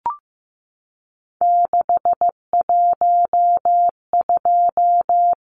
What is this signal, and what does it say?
Morse code, heard as a keyed tone, about 15 wpm, 700 Hz. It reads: 612
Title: Numbers - 3-Digit 15wpm